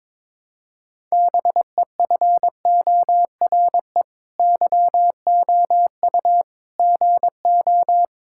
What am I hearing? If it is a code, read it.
Morse code, heard as a keyed tone, about 21 wpm, 700 Hz. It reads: BEFORE YOU GO
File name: Sentences with 9-Word Phrases - 22wpm